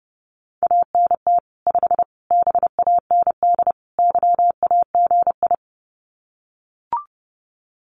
Morse code, heard as a keyed tone, about 28 wpm, 700 Hz. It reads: ANT 5 BAND YAGI
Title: Common QSO Phrases 30wpm